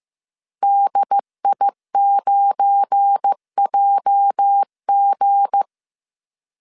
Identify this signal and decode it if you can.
Morse code, heard as a keyed tone, about 15 wpm, 800 Hz. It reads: DI9JG
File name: CW / DET / 1 9 1500C S C 14wpm 3245